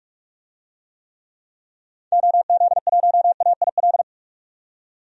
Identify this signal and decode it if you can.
Morse code, heard as a keyed tone, about 44 wpm, 690 Hz. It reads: OZ1AIL
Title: DXCC-100 Call Signs - 6 Characters - 45wpm